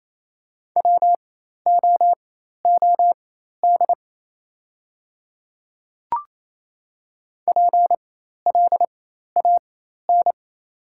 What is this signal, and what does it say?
Morse code, heard as a keyed tone, about 28 wpm, spaced as for 13 wpm, 700 Hz. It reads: WOOD PLAN
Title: 4x Character Spacing - Top 500 Words - 28wpm